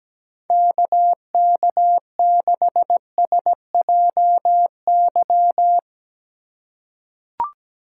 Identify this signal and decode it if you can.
Morse code, heard as a keyed tone, about 17 wpm, 700 Hz. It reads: KK6SJY
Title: USA Call Signs - 6 Characters - 4x Character Spacing - 17wpm